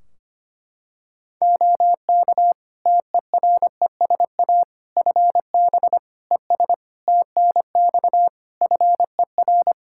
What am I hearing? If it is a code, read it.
Morse code, heard as a keyed tone, about 25 wpm, 700 Hz. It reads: OK TERESA FB ES TNX FER
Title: Practice QSO - Exchange 2 - Info - 25wpm